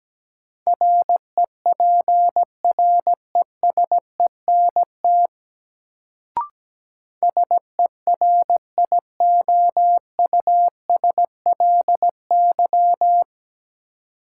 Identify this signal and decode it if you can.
Morse code, heard as a keyed tone, about 17 wpm, 700 Hz. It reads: REPRESENT SERIOUSLY
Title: Sentences from Top 1200 Words - New Word Drill Alt - 17wpm